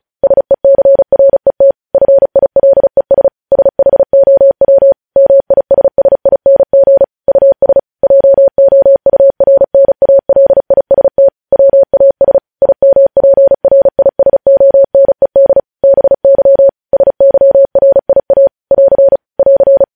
Morse code, heard as a keyed tone, about 36 wpm, 550 Hz. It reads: SECRET FILES SHOW MISSING US JOURNALIST WAS IMPRISONED BY SYRIA <AR> <AR>
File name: podcast / News headlines-1x-35x35